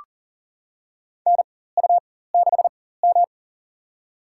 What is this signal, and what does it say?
Morse code, heard as a keyed tone, about 40 wpm, spaced as for 19 wpm, 700 Hz. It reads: NU6M
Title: USA Call Signs - 4 Characters - 4x Character Spacing - 40wpm